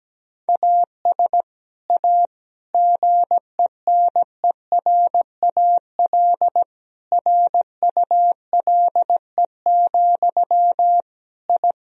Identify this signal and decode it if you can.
Morse code, heard as a keyed tone, about 17 wpm, 700 Hz. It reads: AS A GENERAL RULE, I
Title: Sentences from Top 1000 Words - Review Part 3 - 17wpm